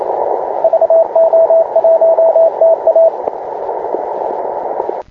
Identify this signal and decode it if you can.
Morse code, heard as a keyed tone, about 27 wpm, 660 Hz, with radio noise. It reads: VO1TA